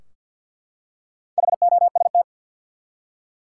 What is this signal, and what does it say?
Morse code, heard as a keyed tone, about 52 wpm, 700 Hz. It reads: HOST
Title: Four Letter Words - 50wpm